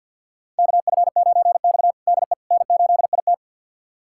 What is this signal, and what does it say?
Morse code, heard as a keyed tone, about 52 wpm, 700 Hz. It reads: KF9X DE N8IT
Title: USA Callsign DE Callsign - 4 Characters - 50wpm